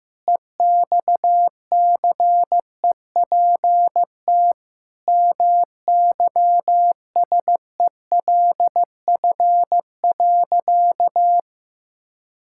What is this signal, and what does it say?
Morse code, heard as a keyed tone, about 15 wpm, 700 Hz. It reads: EXCEPT MYSELF.
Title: Sentences from Top 1600 Words 15wpm